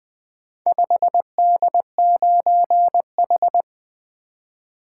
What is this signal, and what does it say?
Morse code, heard as a keyed tone, about 20 wpm, 700 Hz. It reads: 5D9H
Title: Four Letter-Number Combinations - 20wpm